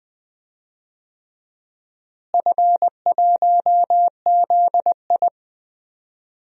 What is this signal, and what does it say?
Morse code, heard as a keyed tone, about 20 wpm, 700 Hz. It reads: F1ZI
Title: DXCC-100 Call Signs - 4 Characters - Repeated 3x - 20wpm